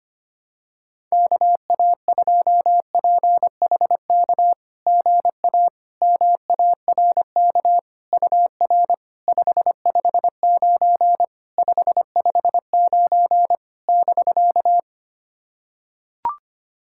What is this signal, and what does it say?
Morse code, heard as a keyed tone, about 25 wpm, 690 Hz. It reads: KA2PHK GA MARK UR 559 559 <BK>
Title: SOTA QSO Practice 25wpm